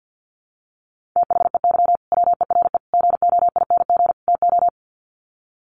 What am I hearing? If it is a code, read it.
Morse code, heard as a keyed tone, about 48 wpm, 700 Hz. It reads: THEY WERE GOING TO
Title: Sets of 4 Words 50wpm